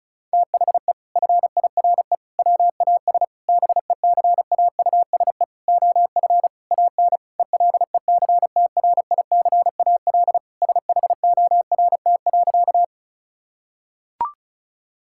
Morse code, heard as a keyed tone, about 36 wpm, 700 Hz. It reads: THE FIRE WAS BECAUSE OF AN ELECTRICAL SHORT.
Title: Sentences from Top 1000 Words - Review Part 6 - 35wpm